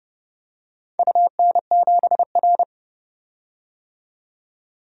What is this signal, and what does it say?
Morse code, heard as a keyed tone, about 31 wpm, 700 Hz. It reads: UN7R
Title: DXCC-100 Call Signs - 4 Characters - 30wpm